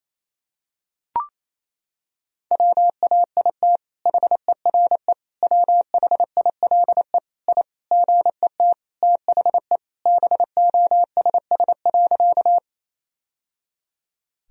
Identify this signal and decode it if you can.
Morse code, heard as a keyed tone, about 28 wpm, 700 Hz. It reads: WAIT HERE WHILE I GET THE BOSS.
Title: Sentences from Top 1000 Words - Review Part 4 - 28wpm